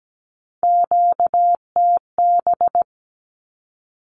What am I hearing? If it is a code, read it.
Morse code, heard as a keyed tone, about 17 wpm, 700 Hz. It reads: QTB
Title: Q-Codes 17wpm